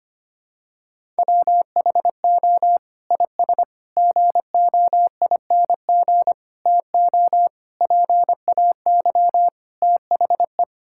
Morse code, heard as a keyed tone, about 25 wpm, 700 Hz. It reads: WHO IS GOING TO PAY THE
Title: Sentences from Top 1000 Words - Review Part 5 - 25wpm